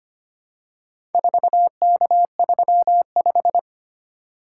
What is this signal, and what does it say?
Morse code, heard as a keyed tone, about 25 wpm, 700 Hz. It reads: 4K35
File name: practice / 25wpm / Four Letter-Number Combinations - 25wpm